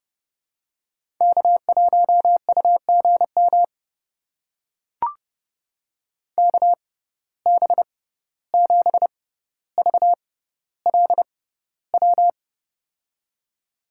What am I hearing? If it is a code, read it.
Morse code, heard as a keyed tone, about 30 wpm, spaced as for 15 wpm, 700 Hz. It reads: K1UGM KB7VLW
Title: USA Call Signs - 6x Character Spacing - 30wpm